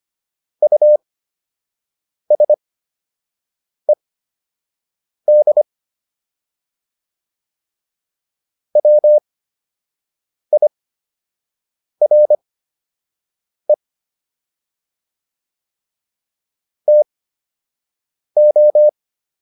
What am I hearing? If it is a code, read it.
Morse code, heard as a keyed tone, about 25 wpm, spaced as for 6 wpm, 600 Hz. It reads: USED WIRE TO